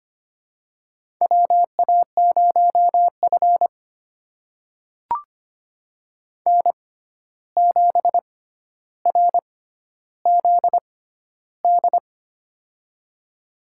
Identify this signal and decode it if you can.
Morse code, heard as a keyed tone, about 25 wpm, spaced as for 12 wpm, 700 Hz. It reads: WA0F N7RZD
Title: USA Call Signs - 6x Character Spacing - 25wpm